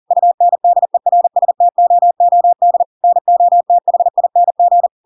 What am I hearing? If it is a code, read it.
Morse code, heard as a keyed tone, about 40 wpm, 700 Hz. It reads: UNDERSTOOD NOTHING
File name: Pinocchio / Pinocchio ch7 40 40 ik1hge